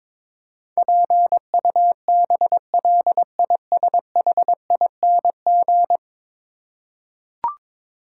Morse code, heard as a keyed tone, about 22 wpm, 700 Hz. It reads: PUBLISHING
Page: morse-code-ninja-practice-22wpm